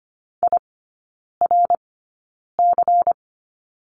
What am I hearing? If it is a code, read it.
Morse code, heard as a keyed tone, about 25 wpm, spaced as for 9 wpm, 700 Hz. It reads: IRC
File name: course / 25:10wpm / 093 Lesson 16e - Letter c - Three Characters 25wpm